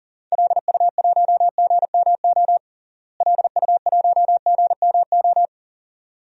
Morse code, heard as a keyed tone, about 40 wpm, 700 Hz. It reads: LU1GMO LU1GMO
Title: DXCC-100 Call Signs - 6 Characters - Repeated 2x - 40wpm